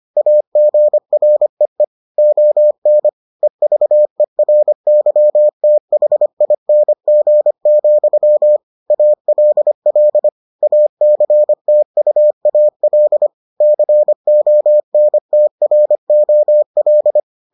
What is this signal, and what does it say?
Morse code, heard as a keyed tone, about 25 wpm, 600 Hz. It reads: AGREE ON EVERYTHING, ALL ACTUAL CONTROL